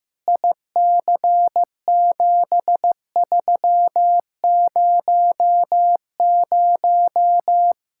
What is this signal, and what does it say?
Morse code, heard as a keyed tone, about 15 wpm, 700 Hz. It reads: IC7300
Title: Practice QSO - Exchange 2 - Info - 15wpm